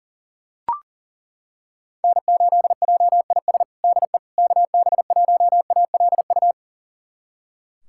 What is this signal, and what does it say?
Morse code, heard as a keyed tone, about 40 wpm, 700 Hz. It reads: N8JIS DE KB1ALU
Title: USA Callsign DE Callsign 40wpm